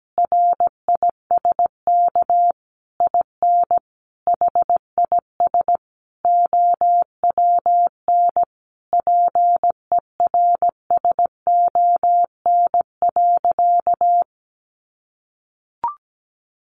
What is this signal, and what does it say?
Morse code, heard as a keyed tone, about 17 wpm, 700 Hz. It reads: RISK IN HIS OWN PERSON.
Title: Sentences from Top 600 Words 17wpm